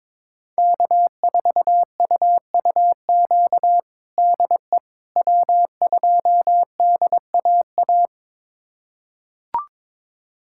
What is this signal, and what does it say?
Morse code, heard as a keyed tone, about 22 wpm, 700 Hz. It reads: K4UUQ DE W2DAA
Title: USA Callsign DE Callsign - 5 Characters - 22wpm